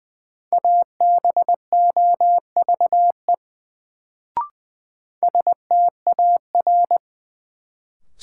Morse code, heard as a keyed tone, about 21 wpm, 700 Hz. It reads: ABOVE STAR
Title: Sentences from Top 1000 Words - Review Part 2 - New Word Drill - 20wpm